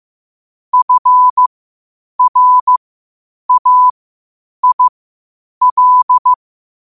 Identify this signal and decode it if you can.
Morse code, heard as a keyed tone, about 15 wpm, spaced as for 8 wpm, 1000 Hz. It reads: FRAIL
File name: qotdfiles / 8wpm